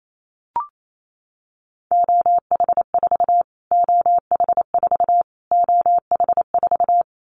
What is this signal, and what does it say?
Morse code, heard as a keyed tone, about 28 wpm, 700 Hz. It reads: OH4 OH4 OH4
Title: Callsign Prefixes for Top 25 DXCC - Repeated 3x - 28wpm